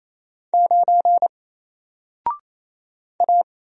Morse code, heard as a keyed tone, about 27 wpm, 700 Hz. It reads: 9 A
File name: Single-Letter-Number-28wpm